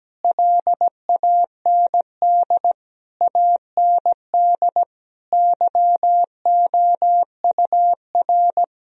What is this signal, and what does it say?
Morse code, heard as a keyed tone, about 17 wpm, 700 Hz. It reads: LAND AND YOUR
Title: Sentences from Top 1000 Words - Review Part 4 - 17wpm